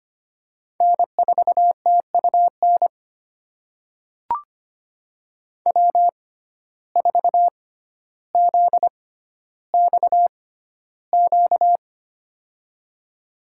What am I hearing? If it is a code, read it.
Morse code, heard as a keyed tone, about 25 wpm, spaced as for 13 wpm, 700 Hz. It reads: N4TUN W4ZXQ
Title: USA Call Signs - 6x Character Spacing - 25wpm